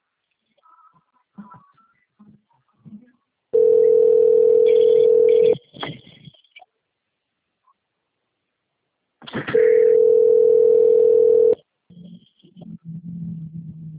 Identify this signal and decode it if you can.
Ringback tone (North America), 440 + 480 Hz.
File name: out- - - - - .41329